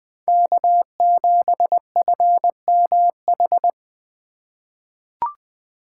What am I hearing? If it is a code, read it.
Morse code, heard as a keyed tone, about 20 wpm, 700 Hz. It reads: K7FMH
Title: USA Call Signs - 5 Characters - 2x Character Spacing - 20wpm